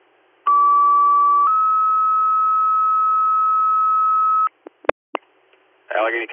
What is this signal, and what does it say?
Two-tone page, heard as a one-second tone, then a three-second tone, about 1150 then 1290 Hz, with speech. It is heard from 0.5 to 4.5 s.